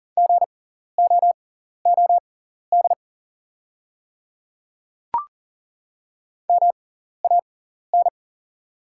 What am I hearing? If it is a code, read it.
Morse code, heard as a keyed tone, about 40 wpm, spaced as for 14 wpm, 700 Hz. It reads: GOOD MAN
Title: 6x Character Spacing - Top 300 Words - 40wpm